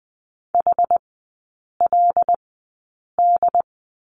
Morse code, heard as a keyed tone, about 20 wpm, spaced as for 8 wpm, 700 Hz. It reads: HLD
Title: 093 Lesson 16e - Letter c - Three Characters 20wpm